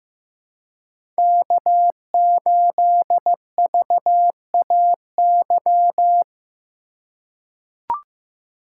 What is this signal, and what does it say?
Morse code, heard as a keyed tone, about 15 wpm, 700 Hz. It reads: K8VAY
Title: USA Call Signs - 5 Characters - 2x Character Spacing - 15wpm